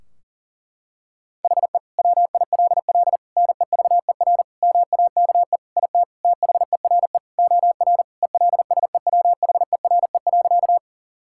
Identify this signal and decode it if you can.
Morse code, heard as a keyed tone, about 40 wpm, 700 Hz. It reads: HE WILL NEVER MAKE IT THERE OR ELSEWHERE.
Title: Sentences from Top 1600 Words 40wpm